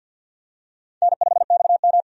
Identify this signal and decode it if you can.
Morse code, heard as a keyed tone, about 52 wpm, 700 Hz. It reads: N5XM
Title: USA Call Signs - 4 Characters - 50wpm